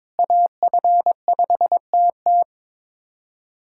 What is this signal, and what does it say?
Morse code, heard as a keyed tone, about 22 wpm, 700 Hz. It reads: AF5TT
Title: USA Callsign DE Callsign - 5 Characters - 22wpm